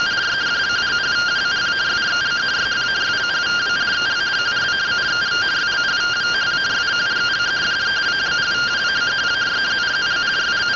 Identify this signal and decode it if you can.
Two-tone teleprinter signal, about 170 Hz shift.